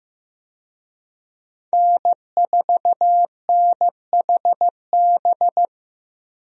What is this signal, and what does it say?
Morse code, heard as a keyed tone, about 15 wpm, 700 Hz. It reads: N4NHB
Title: USA Call Signs - 5 Characters - 15wpm